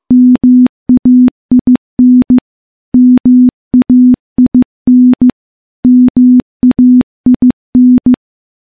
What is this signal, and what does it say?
Morse code, heard as a keyed tone, about 15 wpm, 250 Hz. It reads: MAIN MAIN MAIN
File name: morse-training-15WPM